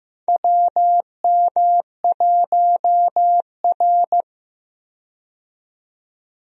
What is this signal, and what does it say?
Morse code, heard as a keyed tone, about 15 wpm, 700 Hz. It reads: WM1R
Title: USA Call Signs - 4 Characters - 15wpm